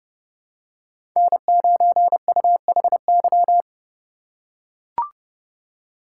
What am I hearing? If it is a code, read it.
Morse code, heard as a keyed tone, about 30 wpm, 700 Hz. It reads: N9UHY
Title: USA Call Signs - 5 Characters - 2x Character Spacing - 30wpm